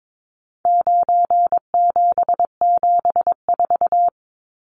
Morse code, heard as a keyed tone, about 22 wpm, 700 Hz. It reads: 9774
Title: Numbers - 4-Digit 22wpm